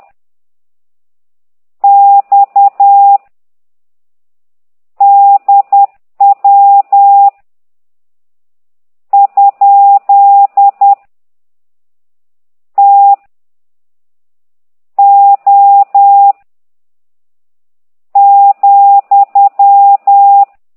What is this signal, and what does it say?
Morse code, heard as a keyed tone, about 10 wpm, 800 Hz. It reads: X DW ? T O ,